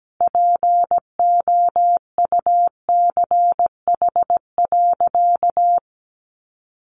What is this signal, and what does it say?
Morse code, heard as a keyed tone, about 17 wpm, 700 Hz. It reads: POUCH.
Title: Easy Phrases 17wpm